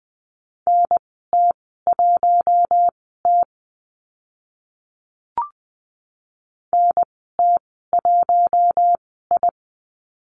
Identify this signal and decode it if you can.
Morse code, heard as a keyed tone, about 20 wpm, spaced as for 15 wpm, 700 Hz. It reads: NT1T NT1I
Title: 042 Lesson 8g - Number 1 - Call Signs 20wpm